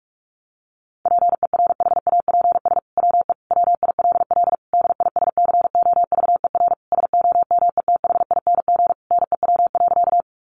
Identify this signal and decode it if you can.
Morse code, heard as a keyed tone, about 44 wpm, 700 Hz. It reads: PERHAPS WE WILL DISCOVER SOMETHING NEW.
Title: Sentences from Top 1100 Words 45wpm